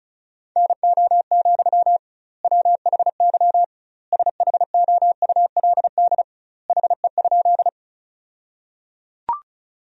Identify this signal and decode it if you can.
Morse code, heard as a keyed tone, about 36 wpm, 700 Hz. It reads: NO, WHY SHOULD HE?
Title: 2x Word-Spacing - Sentences from Top 300 Words - 35wpm